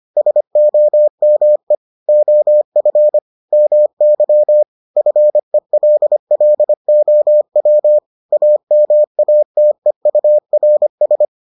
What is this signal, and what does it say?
Morse code, heard as a keyed tone, about 25 wpm, 600 Hz. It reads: SOME OF MY FELLOW AMATEURS